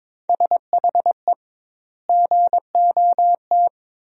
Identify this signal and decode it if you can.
Morse code, heard as a keyed tone, about 21 wpm, 700 Hz. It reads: SHE GOT
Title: 2x Word-Spacing - Sentences from Top 500 Words - 22wpm